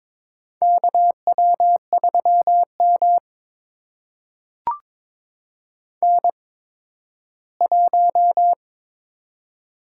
Morse code, heard as a keyed tone, about 21 wpm, spaced as for 13 wpm, 700 Hz. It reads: KW3M N1
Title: USA Call Signs - 4 Characters - 8x Character Spacing - 22wpm